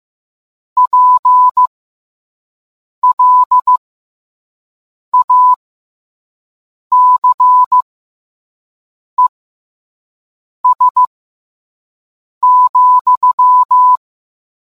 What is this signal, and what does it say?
Morse code, heard as a keyed tone, about 15 wpm, spaced as for 5 wpm, 1000 Hz. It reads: PLACES,